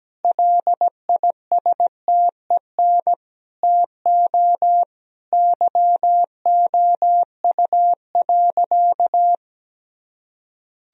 Morse code, heard as a keyed tone, about 17 wpm, 700 Hz. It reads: LISTEN TO YOU.
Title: Sentences from Top 1000 Words - Review Part 3 - 17wpm